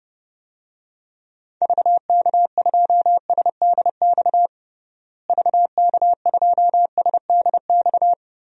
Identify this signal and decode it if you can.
Morse code, heard as a keyed tone, about 31 wpm, 700 Hz. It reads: VK2SDX VK2SDX
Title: DXCC-100 Call Signs - 6 Characters - Repeated 3x - 30wpm